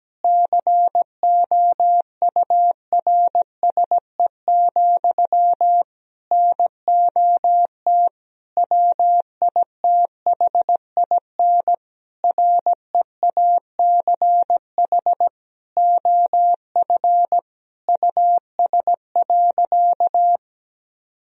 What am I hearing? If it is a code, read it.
Morse code, heard as a keyed tone, about 17 wpm, 700 Hz. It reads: COURSE, NOT WITHIN REACH OF US.
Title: Sentences from Top 1000 Words - Review Part 4 - 17wpm